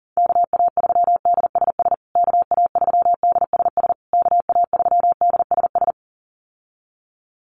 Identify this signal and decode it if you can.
Morse code, heard as a keyed tone, about 40 wpm, 700 Hz. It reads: KA3DSS KA3DSS KA3DSS
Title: USA Call Signs Repeated 3x 40wpm